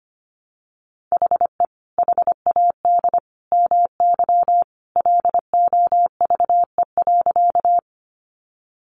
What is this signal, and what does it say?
Morse code, heard as a keyed tone, about 25 wpm, 700 Hz. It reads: HE HAD MY LOVE.
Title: Sentences from Top 400 Words 25wpm